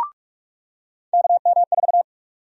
Morse code, heard as a keyed tone, about 44 wpm, 700 Hz. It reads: KM4
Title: USA Call Sign Prefixes - 45wpm